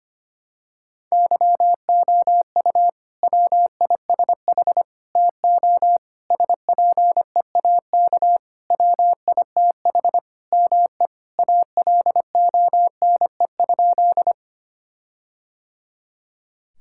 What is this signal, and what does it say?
Morse code, heard as a keyed tone, about 25 wpm, 700 Hz. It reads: YOU WISH TO SPEAK WITH ME ALONE?
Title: Sentences from Top 1600 Words 25wpm